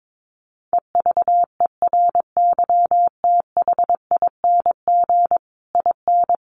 Morse code, heard as a keyed tone, about 22 wpm, 700 Hz. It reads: EVERYTHING IN
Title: A Sound Of Thunder 22wpm